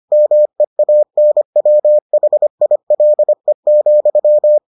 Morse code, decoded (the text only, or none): MEANWHILE,